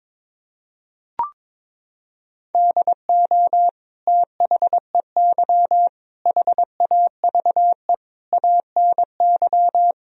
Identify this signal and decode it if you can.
Morse code, heard as a keyed tone, about 22 wpm, 700 Hz. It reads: DO THEY HAVE ANY
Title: Sentences from Top 1000 Words - Review Part 3 - 22wpm